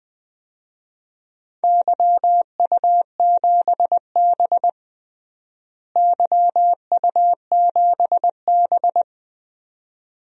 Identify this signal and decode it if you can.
Morse code, heard as a keyed tone, about 20 wpm, 700 Hz. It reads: YU7B YU7B
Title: DXCC-100 Call Signs - 4 Characters - Repeated 2x - 20wpm